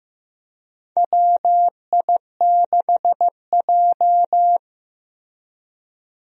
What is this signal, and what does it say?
Morse code, heard as a keyed tone, about 15 wpm, 700 Hz. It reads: WI6J
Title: USA Call Signs - 4 Characters - Repeated 2x - 15wpm